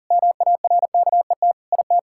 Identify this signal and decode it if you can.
Morse code, heard as a keyed tone, about 40 wpm, 700 Hz. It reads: MARKET IT
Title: Sentences from Top 1600 Words 40wpm